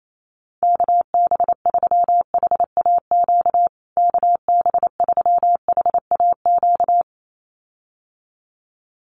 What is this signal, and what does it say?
Morse code, heard as a keyed tone, about 28 wpm, 700 Hz. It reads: KB3HAQ KB3HAQ
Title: USA Call Signs Repeated 2x 28wpm